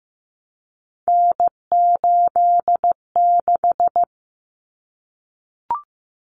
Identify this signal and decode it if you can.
Morse code, heard as a keyed tone, about 15 wpm, 700 Hz. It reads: N86
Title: Common Cut Numbers - 3-Digit 15wpm